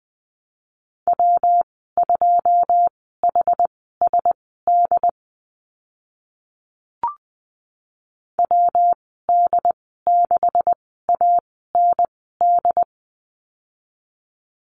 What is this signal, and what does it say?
Morse code, heard as a keyed tone, about 20 wpm, spaced as for 15 wpm, 700 Hz. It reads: W2HSD WD6AND
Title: 140 Lesson 23g - Letter f - Call Signs 20wpm